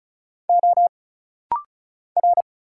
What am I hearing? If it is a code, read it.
Morse code, heard as a keyed tone, about 35 wpm, 700 Hz. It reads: O R
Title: Single Letters - Rapid-Fire 35wpm